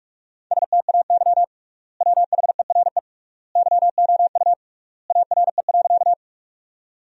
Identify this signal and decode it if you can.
Morse code, heard as a keyed tone, about 46 wpm, 700 Hz. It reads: STAY WHERE YOU ARE.
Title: 3x Word-Spacing - Sentences from Top 500 Words - 45wpm